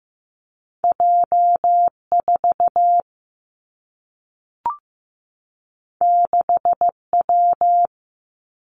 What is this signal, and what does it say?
Morse code, heard as a keyed tone, about 15 wpm, 700 Hz. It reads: J 4 6 W
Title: Two Challenging Letter-Number Combinations 15wpm